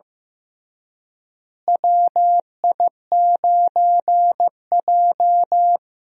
Morse code, heard as a keyed tone, about 15 wpm, 700 Hz. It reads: WI9J
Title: USA Call Signs - 4 Characters - Repeated 3x - 15wpm